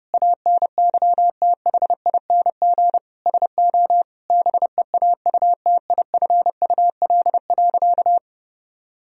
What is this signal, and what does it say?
Morse code, heard as a keyed tone, about 32 wpm, 700 Hz. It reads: ANYTHING SO BEAUTIFUL.
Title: Sentences from Top 1000 Words - Review Part 1 - 30wpm